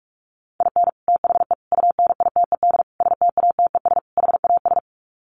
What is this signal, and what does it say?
Morse code, heard as a keyed tone, about 44 wpm, 700 Hz. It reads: IN THE UNITED STATES HAS
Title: Sets of 5 Words Encore2 45wpm